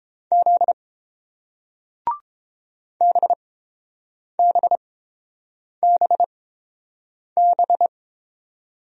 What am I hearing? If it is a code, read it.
Morse code, heard as a keyed tone, about 30 wpm, spaced as for 7 wpm, 700 Hz. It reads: Z BBBB